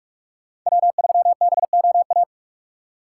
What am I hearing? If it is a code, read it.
Morse code, heard as a keyed tone, about 53 wpm, 700 Hz. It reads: W3BOA
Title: Speed Racing - USA 1x3 Call Signs 45wpm